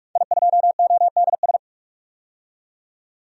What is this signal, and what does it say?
Morse code, heard as a keyed tone, about 42 wpm, 700 Hz. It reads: I2ODS
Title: DXCC-100 Call Signs - 5 Characters - 45wpm